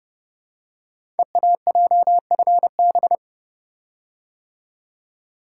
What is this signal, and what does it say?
Morse code, heard as a keyed tone, about 30 wpm, 700 Hz. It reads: EAJFB
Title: Code Groups - 5 Letter:Numbers - 30wpm